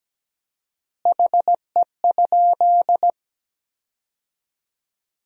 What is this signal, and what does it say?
Morse code, heard as a keyed tone, about 17 wpm, 700 Hz. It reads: HE?
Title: 4x Word-Spacing - Sentences from Top 100 Words - 17wpm